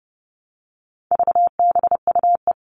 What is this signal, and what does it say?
Morse code, heard as a keyed tone, about 30 wpm, 700 Hz. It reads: VBUE
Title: All Cut Numbers - 4-Digit 30wpm